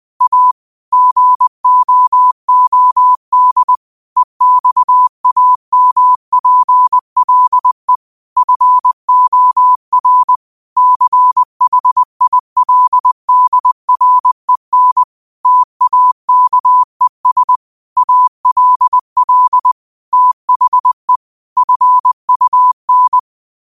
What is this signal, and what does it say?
Morse code, heard as a keyed tone, about 20 wpm, 1000 Hz. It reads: A GOOD EXAMPLE FOR CHILDREN TAKES ALL THE FUN
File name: qotdfiles / 20wpm